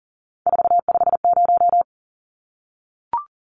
Morse code, heard as a keyed tone, about 40 wpm, 700 Hz. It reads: 450
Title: Numbers - 3-Digit 40wpm